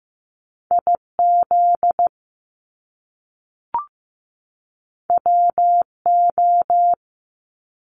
Morse code, heard as a keyed tone, about 15 wpm, 700 Hz. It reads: IZ WO
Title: Top Two Letter Combinations in Words 15wpm